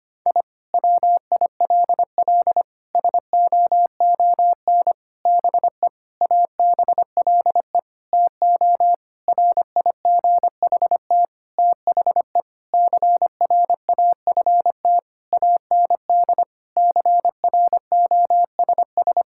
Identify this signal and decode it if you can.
Morse code, heard as a keyed tone, about 25 wpm, 700 Hz. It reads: I WILL SOON BE ABLE TO RIGHT THE CRAFT AND CROSS
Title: Sentences from Top 1000 Words - Review Part 6 - 25wpm